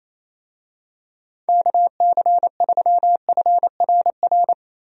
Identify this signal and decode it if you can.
Morse code, heard as a keyed tone, about 27 wpm, 700 Hz. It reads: KC3FRR
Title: USA Call Signs - 6 Characters - 28wpm